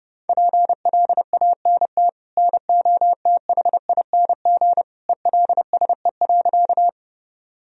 Morse code, heard as a keyed tone, about 30 wpm, 700 Hz. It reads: PLANT NOTHING ELSE.
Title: Sentences from Top 1000 Words - Review Part 1 - 30wpm